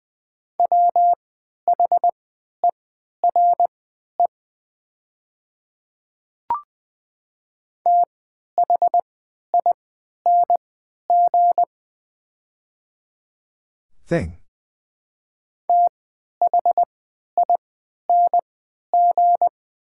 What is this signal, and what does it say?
Morse code, heard as a keyed tone, about 20 wpm, spaced as for 11 wpm, 700 Hz. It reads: WHERE THING THING
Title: 3x Character Spacing - Top 200 Words - 20wpm